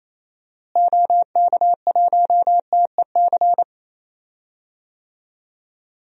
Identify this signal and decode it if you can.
Morse code, heard as a keyed tone, about 28 wpm, 700 Hz. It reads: OK1TEC
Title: DXCC-100 Call Signs - 6 Characters - Repeated 3x - 28wpm